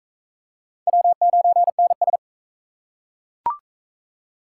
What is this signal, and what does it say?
Morse code, heard as a keyed tone, about 34 wpm, 700 Hz. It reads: W9NS
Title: Speed Racing - USA 1x2 Call Signs 28wpm